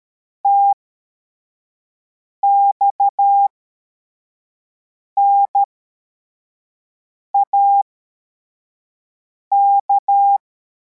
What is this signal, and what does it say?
Morse code, heard as a keyed tone, about 13 wpm, spaced as for 4 wpm, 800 Hz. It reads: TXNAK
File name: formation / cw / Text25